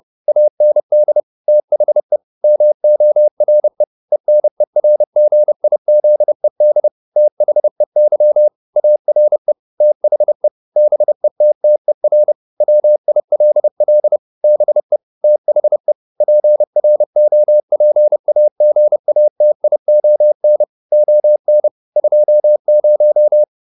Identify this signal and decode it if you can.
Morse code, heard as a keyed tone, about 30 wpm, 600 Hz. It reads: AND THE MORE ENERGIZED THEY ARE THE BETTER WILL BE THE PROPAGATION ON 20